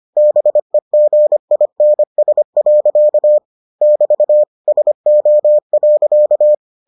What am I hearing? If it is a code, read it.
Morse code, heard as a keyed tone, about 25 wpm, 600 Hz. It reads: BEGINS. <BT> SO.